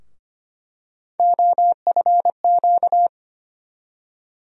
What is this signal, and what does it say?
Morse code, heard as a keyed tone, about 25 wpm, 700 Hz. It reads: OFQ
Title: USA Call Sign Postfixes - 3 Letters 25wpm